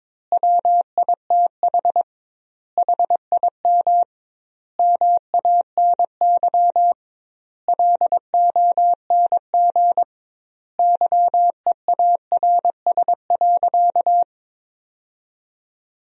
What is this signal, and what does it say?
Morse code, heard as a keyed tone, about 22 wpm, 700 Hz. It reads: WITH HIM MANY LONG YEARS.
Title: 2x Word-Spacing - Sentences from Top 200 Words - 22wpm